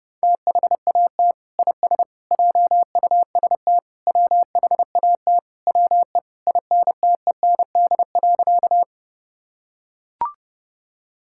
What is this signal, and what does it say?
Morse code, heard as a keyed tone, about 30 wpm, 700 Hz. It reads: THAT IS JUST WHAT WE INTEND.
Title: Sentences from Top 1600 Words 30wpm